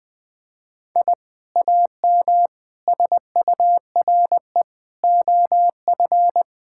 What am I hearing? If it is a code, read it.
Morse code, heard as a keyed tone, about 20 wpm, 700 Hz. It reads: I AM SURE OF
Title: Sentences with 2-Word Phrases - 20wpm